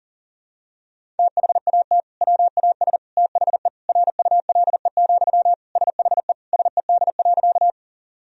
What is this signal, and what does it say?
Morse code, heard as a keyed tone, about 37 wpm, 700 Hz. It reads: THAT WAS THE RULE, SHE SED.
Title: Sentences from Top 400 Words and Abbreviations 40wpm